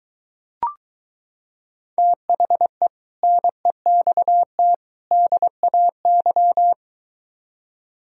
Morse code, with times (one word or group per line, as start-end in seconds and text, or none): 2.0-2.9 THE
3.2-4.7 NEXT
5.1-6.7 DAY